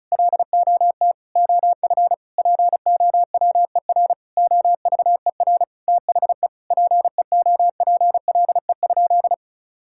Morse code, heard as a keyed tone, about 34 wpm, 700 Hz. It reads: LOT OF POWER OVER THE PEOPLE?
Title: Sentences from Top 1000 Words - Review Part 5 - 35wpm